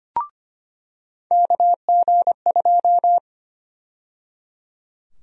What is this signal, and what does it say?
Morse code, heard as a keyed tone, about 25 wpm, 700 Hz. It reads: KG2
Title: USA Call Sign Prefixes - 25wpm